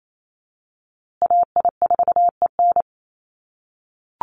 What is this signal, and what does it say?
Morse code, heard as a keyed tone, about 28 wpm, 700 Hz. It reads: AI4EN
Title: USA Call Signs Repeated 2x 28wpm